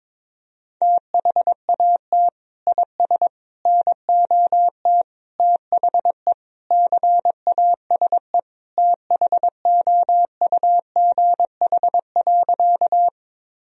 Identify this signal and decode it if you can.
Morse code, heard as a keyed tone, about 22 wpm, 700 Hz. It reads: THAT IS NOT THE CASE THOUGH.
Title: Sentences from Top 1000 Words - Review Part 2 - 22wpm